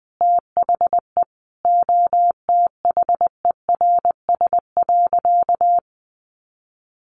Morse code, heard as a keyed tone, about 20 wpm, 700 Hz. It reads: THE OTHERS.
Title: Sentences from Top 1500 Words 20wpm